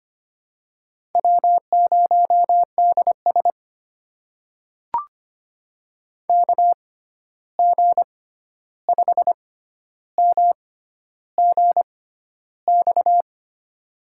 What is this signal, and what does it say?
Morse code, heard as a keyed tone, about 25 wpm, spaced as for 11 wpm, 700 Hz. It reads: W0DS KG5MGX
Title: USA Call Signs - 6x Character Spacing - 25wpm